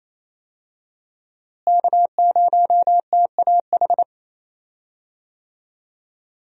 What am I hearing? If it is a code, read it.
Morse code, heard as a keyed tone, about 27 wpm, 700 Hz. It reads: K0TAH
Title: USA Call Signs - 5 Characters - Repeated 3x - 28wpm